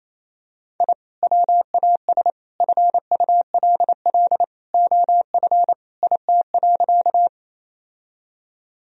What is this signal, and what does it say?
Morse code, heard as a keyed tone, about 27 wpm, 700 Hz. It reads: I WAS FULL OF IT.
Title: Sentences from Top 1000 Words - Review Part 3 - 28wpm